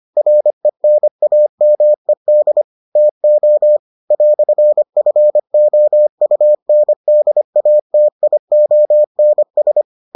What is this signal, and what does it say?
Morse code, heard as a keyed tone, about 25 wpm, 600 Hz. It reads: RENAMED TO "FOUNDATIONS